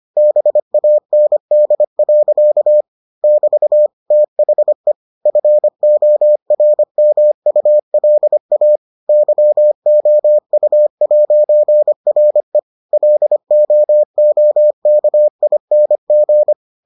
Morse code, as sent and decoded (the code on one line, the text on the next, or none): -... .- -. -.. .-.-.- / -...- / - .... / ..-. --- .-. -- ..- .-.. .- / -.-- --- ..- .----. .-. / .-.. --- --- -.- .. -. --.
BAND. <BT> THE FORMULA YOU'RE LOOKING